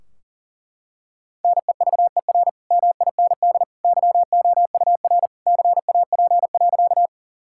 Morse code, heard as a keyed tone, about 40 wpm, 700 Hz. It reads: NEVER MIND YOUR CAP.
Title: Sentences from Top 1600 Words 40wpm